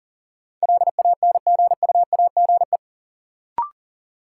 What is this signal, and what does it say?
Morse code, heard as a keyed tone, about 37 wpm, 700 Hz. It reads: LANGUAGE
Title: Sentences from Top 500 Words - New Word Drill - 40wpm